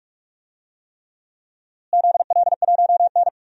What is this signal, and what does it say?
Morse code, heard as a keyed tone, about 42 wpm, 700 Hz. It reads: ZL1N
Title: DXCC-100 Call Signs - 4 Characters - Repeated 3x - 45wpm